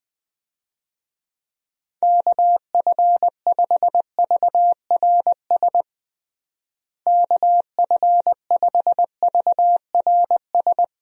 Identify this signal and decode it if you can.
Morse code, heard as a keyed tone, about 21 wpm, 700 Hz. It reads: KF5VRS KF5VRS
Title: USA Call Signs - 6 Characters - Repeated 2x - 20wpm